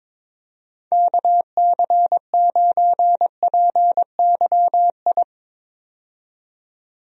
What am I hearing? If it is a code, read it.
Morse code, heard as a keyed tone, about 21 wpm, 700 Hz. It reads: KC9PYI
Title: USA Call Signs - 6 Characters - 22wpm